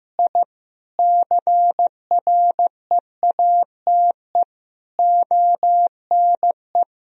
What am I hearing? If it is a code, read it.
Morse code, heard as a keyed tone, about 15 wpm, 700 Hz. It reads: I CREATE ONE
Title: Sentences from Top 1000 Words - Review Part 3 - 15wpm